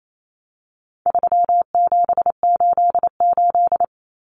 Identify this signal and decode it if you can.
Morse code, heard as a keyed tone, about 28 wpm, 700 Hz. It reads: 3788
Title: Common Cut Numbers - 4-Digit 28wpm